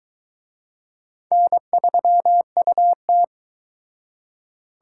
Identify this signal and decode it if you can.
Morse code, heard as a keyed tone, about 23 wpm, 700 Hz. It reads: N3UT